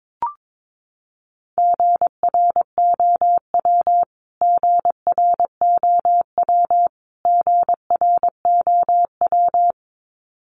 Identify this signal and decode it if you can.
Morse code, heard as a keyed tone, about 22 wpm, 700 Hz. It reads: GROW GROW GROW
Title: Top 300 Words Repeated 22wpm